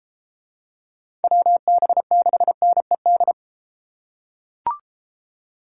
Morse code, heard as a keyed tone, about 31 wpm, 700 Hz. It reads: WB6NED